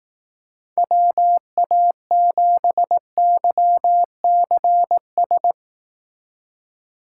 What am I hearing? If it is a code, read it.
Morse code, heard as a keyed tone, about 18 wpm, 700 Hz. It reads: WA7YCS
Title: Speed Racing - USA 2x3 Call Signs 15wpm